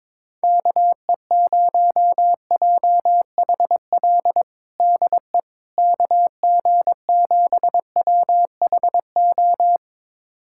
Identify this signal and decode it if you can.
Morse code, heard as a keyed tone, about 22 wpm, 700 Hz. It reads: KE0JHL DE KG7WHO
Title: USA Callsign DE Callsign 22wpm